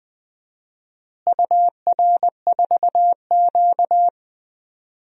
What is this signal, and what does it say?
Morse code, heard as a keyed tone, about 21 wpm, 700 Hz. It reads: UR4Q
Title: DXCC-100 Call Signs - 4 Characters - Repeated 2x - 20wpm